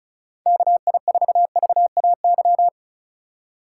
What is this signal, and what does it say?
Morse code, heard as a keyed tone, about 36 wpm, 700 Hz. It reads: KI4VAY
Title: USA Call Signs - 3x Character Spacing - 35wpm